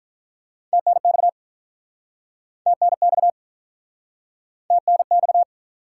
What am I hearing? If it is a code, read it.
Morse code, heard as a keyed tone, about 48 wpm, 700 Hz. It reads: TNX TNX TNX